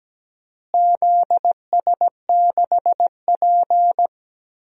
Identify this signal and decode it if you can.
Morse code, heard as a keyed tone, about 17 wpm, 700 Hz. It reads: ZS6P
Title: DXCC-100 Call Signs - 4 Characters - Repeated 3x - 17wpm